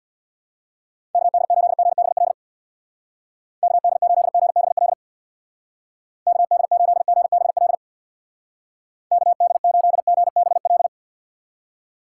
Morse code, heard as a keyed tone, about 65 wpm, 700 Hz. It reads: KD8GBL KD8GBL KD8GBL KD8GBL